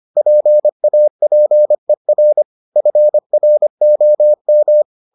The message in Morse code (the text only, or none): PAPER FROM